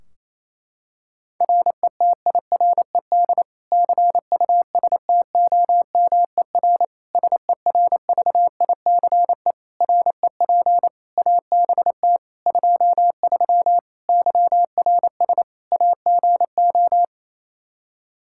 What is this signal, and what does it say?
Morse code, heard as a keyed tone, about 28 wpm, 700 Hz. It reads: RETIRED CUSTOMER SERVICE REP ABT 23 YRS AGO
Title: K1USN SST QSO Practice 28wpm